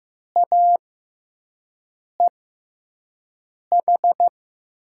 Morse code, heard as a keyed tone, about 15 wpm, spaced as for 5 wpm, 700 Hz. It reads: AEH
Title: USA Call Signs - 5 Characters - 6x Character Spacing - 15wpm